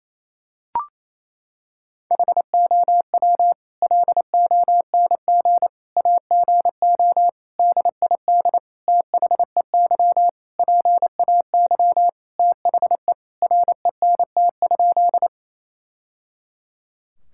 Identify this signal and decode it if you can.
Morse code, heard as a keyed tone, about 28 wpm, 700 Hz. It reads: HOW LONG AGO DID THEY PAY THE RENT?
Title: Sentences from Top 1000 Words - Review Part 5 - 28wpm